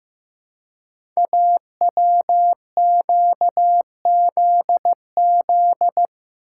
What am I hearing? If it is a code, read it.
Morse code, heard as a keyed tone, about 15 wpm, 700 Hz. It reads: AWQZZ
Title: Code Groups - 5 Letters - 15wpm